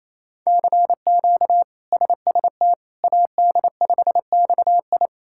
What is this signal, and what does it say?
Morse code, heard as a keyed tone, about 28 wpm, 700 Hz. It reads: CQ SST AD5XI
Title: K1USN SST QSO Practice - 28wpm